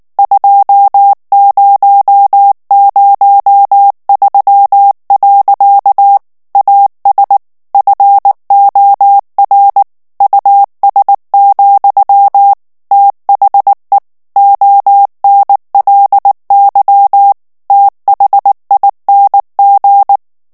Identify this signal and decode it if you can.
Morse code, heard as a keyed tone, about 19 wpm, 800 Hz. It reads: 2003. AS FOR US, THE ONLY THING
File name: morse mp3 / ANGLAIS texte clair 2 vitesse 18 mots minute